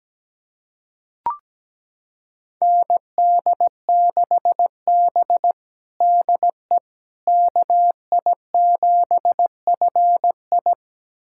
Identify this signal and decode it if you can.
Morse code, heard as a keyed tone, about 17 wpm, 700 Hz. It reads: ND6B DE KI7FI
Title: USA Callsign DE Callsign 17wpm